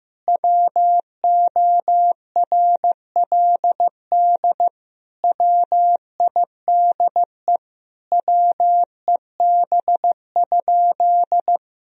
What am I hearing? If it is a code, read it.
Morse code, heard as a keyed tone, about 15 wpm, 700 Hz. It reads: WORLD WIDE WEB?
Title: Sentences from Top 1000 Words - Review Part 4 - 15wpm